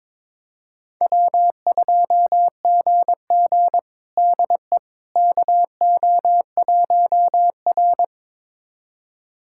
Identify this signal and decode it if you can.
Morse code, heard as a keyed tone, about 22 wpm, 700 Hz. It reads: W2GG DE KO1R
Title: USA Callsign DE Callsign - 4 Characters - 22wpm